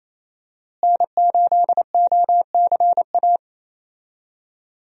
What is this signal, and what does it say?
Morse code, heard as a keyed tone, about 28 wpm, 700 Hz. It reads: N8OCA
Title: USA Call Signs - 5 Characters - Repeated 3x - 28wpm